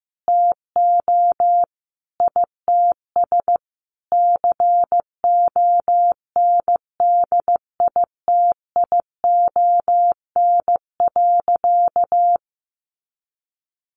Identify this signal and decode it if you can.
Morse code, heard as a keyed tone, about 15 wpm, 700 Hz. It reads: TO ITS CONDITION.
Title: Sentences from Top 1100 Words 15wpm